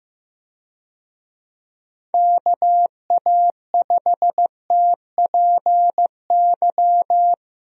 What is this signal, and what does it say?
Morse code, heard as a keyed tone, about 15 wpm, 700 Hz. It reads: KA5TPY
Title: USA Call Signs - 6 Characters - Repeated 3x - 15wpm